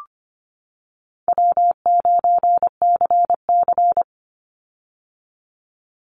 Morse code, heard as a keyed tone, about 25 wpm, 700 Hz. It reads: W9CC
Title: Easy Call Signs - 25wpm